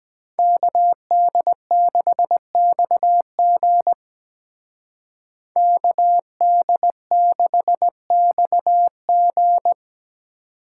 Morse code, heard as a keyed tone, about 20 wpm, 700 Hz. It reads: KD6XG KD6XG
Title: Speed Racing - USA 2x2 Call Signs 17wpm